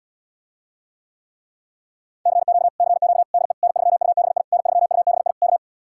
Morse code, heard as a keyed tone, about 77 wpm, 700 Hz. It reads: CQ CQ DE N3UBI N3UBI K